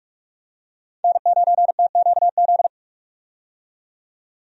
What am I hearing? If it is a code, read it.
Morse code, heard as a keyed tone, about 44 wpm, 700 Hz. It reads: N9TQZ
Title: USA Call Signs - 5 Characters - 45wpm